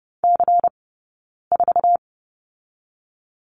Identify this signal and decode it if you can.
Morse code, heard as a keyed tone, about 30 wpm, 700 Hz. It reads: C 4
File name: 092 Lesson 16d - Letter c - Two Characters 30wpm